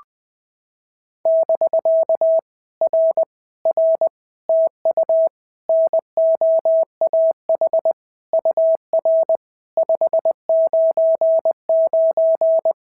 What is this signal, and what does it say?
Morse code, heard as a keyed tone, about 20 wpm, 650 Hz. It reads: <BK> R R TU NOAH UR 599
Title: POTA QSO Practice - 20wpm